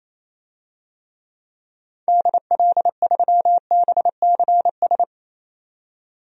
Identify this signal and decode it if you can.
Morse code, heard as a keyed tone, about 28 wpm, 700 Hz. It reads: DL3BCS
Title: DXCC-100 Call Signs - 6 Characters - 28wpm